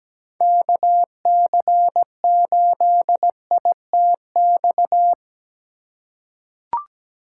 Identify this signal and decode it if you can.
Morse code, heard as a keyed tone, about 17 wpm, 700 Hz. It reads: KC8ITX
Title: USA Call Signs - 6 Characters - 4x Character Spacing - 17wpm